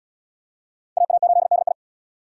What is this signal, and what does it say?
Morse code, heard as a keyed tone, about 74 wpm, 690 Hz. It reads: AA8DI